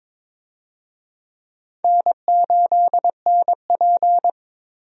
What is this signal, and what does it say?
Morse code, heard as a keyed tone, about 22 wpm, 700 Hz. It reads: N8NP
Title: USA Call Signs - 4 Characters - Repeated 3x - 22wpm